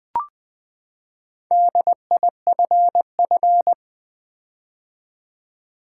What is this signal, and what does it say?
Morse code, heard as a keyed tone, about 20 wpm, 700 Hz. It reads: DIFF
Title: Build A Word - 4 Letters At A Time - 10-Letter Words - 20wpm